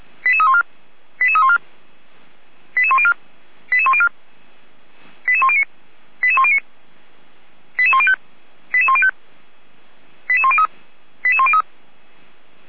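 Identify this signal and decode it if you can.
Five-tone selective call (ZVEI), heard as quick runs of five tones, tones of about 70 ms. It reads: 88314 88314 88104 88104 88109 88109 88105 88105 88103 88103
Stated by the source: Five-tone ZVEI selective call received by SDR.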